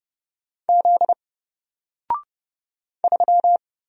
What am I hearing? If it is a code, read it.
Morse code, heard as a keyed tone, about 31 wpm, 700 Hz. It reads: Z 3
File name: Single-Letter-Number-30wpm